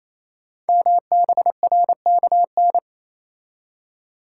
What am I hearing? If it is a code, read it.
Morse code, heard as a keyed tone, about 27 wpm, 700 Hz. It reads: MBRKN
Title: Code Groups - 5 Letters - 28wpm